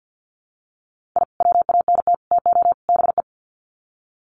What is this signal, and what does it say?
Morse code, heard as a keyed tone, about 50 wpm, 700 Hz. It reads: I WANT TO BE
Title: Sets of 4 Words 50wpm